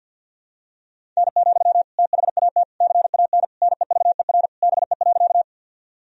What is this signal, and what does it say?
Morse code, heard as a keyed tone, about 47 wpm, 700 Hz. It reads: N, THAT CAN NEVER BE.
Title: Sentences from Top 300 Words and Abbreviations 50wpm